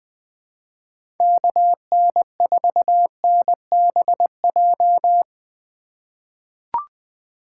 Morse code, heard as a keyed tone, about 20 wpm, 700 Hz. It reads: KN4NBJ
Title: 2023 POTA Activator Call Signs - 20wpm